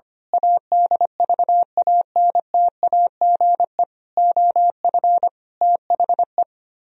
Morse code, heard as a keyed tone, about 25 wpm, 700 Hz. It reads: ADVANTAGE OF THE
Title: Speed-Racing - Sets of 5 Words 25wpm